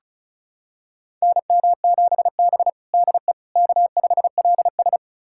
Morse code, heard as a keyed tone, about 36 wpm, 700 Hz. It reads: NM7B DE K5LS
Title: USA Callsign DE Callsign - 4 Characters - 35wpm